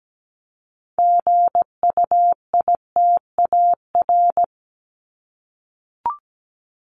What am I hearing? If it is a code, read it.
Morse code, heard as a keyed tone, about 17 wpm, 700 Hz. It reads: GUITAR
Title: Top 1501-1750 Words 17wpm